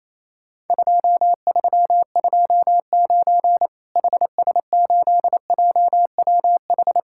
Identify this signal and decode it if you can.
Morse code, heard as a keyed tone, about 27 wpm, 700 Hz. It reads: 2329 HS8JWH
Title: Worked All Europe DX Contest - QTC Exchange - 28wpm